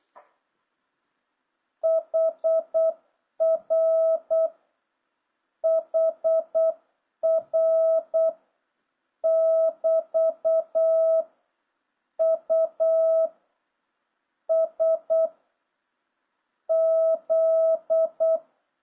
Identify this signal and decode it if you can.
Morse code, heard as a keyed tone, about 8 wpm, 650 Hz. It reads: HR HR <BT> U S Z